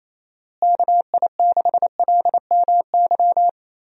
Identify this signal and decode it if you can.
Morse code, heard as a keyed tone, about 28 wpm, 700 Hz. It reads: KI6LMY
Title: USA Call Signs - 6 Characters - Repeated 3x - 28wpm